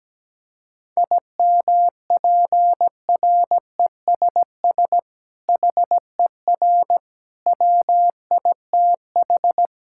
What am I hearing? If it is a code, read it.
Morse code, heard as a keyed tone, about 17 wpm, 700 Hz. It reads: IMPRESS HER WITH
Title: Sentences from Top 1600 Words 17wpm